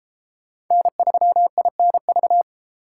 Morse code, heard as a keyed tone, about 32 wpm, 700 Hz. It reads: N3INV